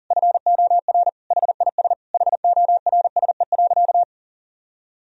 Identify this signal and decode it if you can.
Morse code, heard as a keyed tone, about 40 wpm, 700 Hz. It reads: FOR HIS HORSE.
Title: Sentences from Top 1000 Words - Review Part 3 - 40wpm